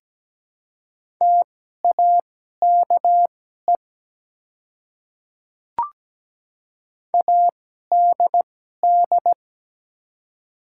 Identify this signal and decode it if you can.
Morse code, heard as a keyed tone, about 17 wpm, spaced as for 12 wpm, 700 Hz. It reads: TAKE ADD
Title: 2x Character Spacing - Top 200 Words - 17wpm